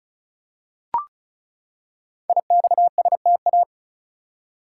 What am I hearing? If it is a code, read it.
Morse code, heard as a keyed tone, about 36 wpm, 700 Hz. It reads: IXSTA
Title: Code Groups - 5 Letters - 35wpm